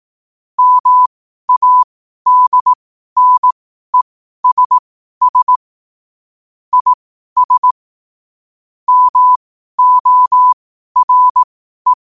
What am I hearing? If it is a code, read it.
Morse code, heard as a keyed tone, about 18 wpm, spaced as for 12 wpm, 1000 Hz. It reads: MADNESS IS MORE
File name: qotdfiles / 12wpm